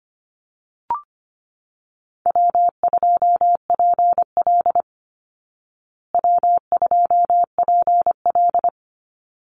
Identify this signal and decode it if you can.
Morse code, heard as a keyed tone, about 25 wpm, 700 Hz. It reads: W2PL W2PL
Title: Call Signs Repeated 3x - 25wpm